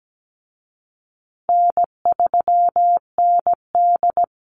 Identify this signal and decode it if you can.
Morse code, heard as a keyed tone, about 17 wpm, 700 Hz. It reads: N3ND
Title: Easy Call Signs Repeated 3x - 17wpm